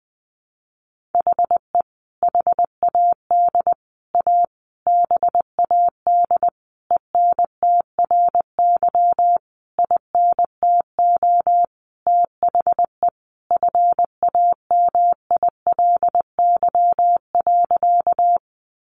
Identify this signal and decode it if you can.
Morse code, heard as a keyed tone, about 19 wpm, 700 Hz. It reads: HE HAD A BAD ENTRY INTO THE FAMILY.
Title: Sentences from Top 1500 Words 20wpm